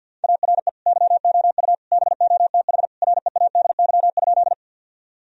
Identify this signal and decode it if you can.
Morse code, heard as a keyed tone, about 52 wpm, 700 Hz. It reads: ARE YOU BOTH READY?
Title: Sentences from Top 1000 Words - Review Part 1 - 50wpm